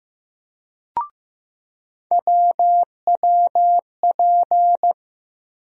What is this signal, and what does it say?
Morse code, heard as a keyed tone, about 15 wpm, 700 Hz. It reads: WWP
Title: USA Call Sign Postfixes - 3 Letters 15wpm